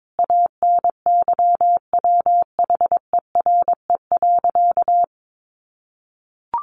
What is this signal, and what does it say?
Morse code, heard as a keyed tone, about 22 wpm, 700 Hz. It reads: ANYWHERE.
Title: Sentences from Top 1400 Words 22wpm